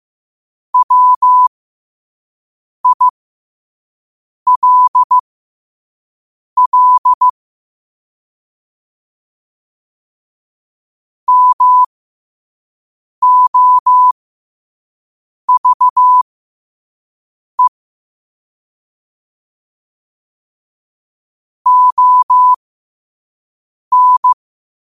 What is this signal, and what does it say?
Morse code, heard as a keyed tone, about 15 wpm, spaced as for 5 wpm, 1000 Hz. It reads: WILL MOVE ON
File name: qotdfiles / 5wpm